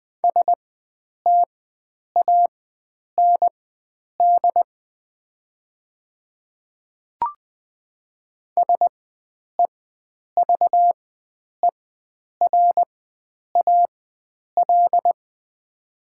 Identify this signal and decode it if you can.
Morse code, heard as a keyed tone, about 20 wpm, spaced as for 9 wpm, 700 Hz. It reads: STAND SEVERAL
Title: 4x Character Spacing - Top 500 Words - 20wpm